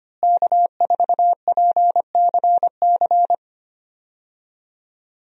Morse code, heard as a keyed tone, about 25 wpm, 700 Hz. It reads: K4PCC
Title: USA Call Signs - 5 Characters - Repeated 3x - 25wpm